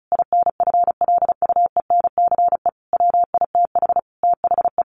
Morse code, heard as a keyed tone, about 34 wpm, 700 Hz. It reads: INFLUENCE WITH THE
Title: Sentences from Top 1100 Words 35wpm